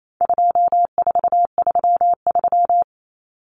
Morse code, heard as a keyed tone, about 28 wpm, 700 Hz. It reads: 2433
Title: Common Cut Numbers - 4-Digit 28wpm